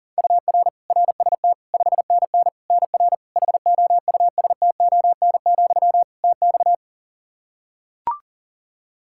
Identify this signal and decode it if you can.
Morse code, heard as a keyed tone, about 40 wpm, 700 Hz. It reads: UR RST 5NN NR HOUSTON, TX
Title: K1USN SST QSO Practice 40wpm